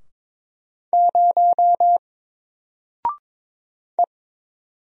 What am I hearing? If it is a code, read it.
Morse code, heard as a keyed tone, about 22 wpm, 700 Hz. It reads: 0 E
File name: Single-Letter-Number-22wpm